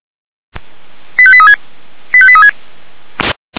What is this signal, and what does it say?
Five-tone selective call (ZVEI), heard as quick runs of five tones, tones of about 80 ms. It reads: 85836 85836